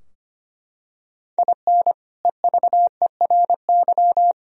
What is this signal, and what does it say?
Morse code, heard as a keyed tone, about 25 wpm, 700 Hz. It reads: IN EVERY
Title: Sentences from Top 1000 Words - Review Part 4 - 25wpm